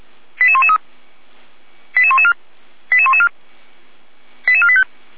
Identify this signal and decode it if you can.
Five-tone selective call (ZVEI), heard as quick runs of five tones, tones of about 70 ms. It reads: 88113 88114 88114 88485
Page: SDR/ZVEI